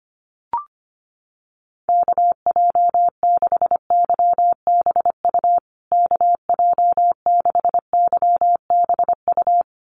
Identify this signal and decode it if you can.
Morse code, heard as a keyed tone, about 25 wpm, 700 Hz. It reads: KJ6YBU KJ6YBU
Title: USA Call Signs Repeated 2x 25wpm